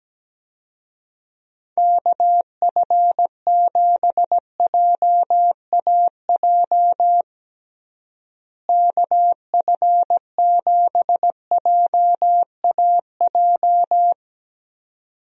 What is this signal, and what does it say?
Morse code, heard as a keyed tone, about 17 wpm, 700 Hz. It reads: KF7JAJ KF7JAJ
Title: USA Call Signs - 6 Characters - Repeated 2x - 17wpm